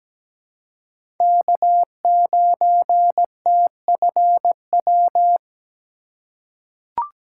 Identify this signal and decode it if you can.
Morse code, heard as a keyed tone, about 17 wpm, 700 Hz. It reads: K9TFW
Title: USA Call Signs - 5 Characters - 4x Character Spacing - 17wpm